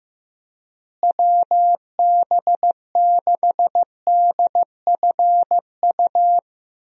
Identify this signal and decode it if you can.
Morse code, heard as a keyed tone, about 15 wpm, 700 Hz. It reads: WB6DFU
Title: USA Call Signs - 6 Characters - Repeated 2x - 15wpm